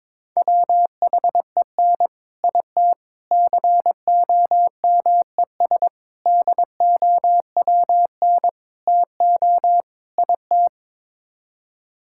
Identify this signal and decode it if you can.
Morse code, heard as a keyed tone, about 22 wpm, 700 Hz. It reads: WHEN IT COMES DOWN TO IT
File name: practice / 22wpm / Sets of 6 Words - 22wpm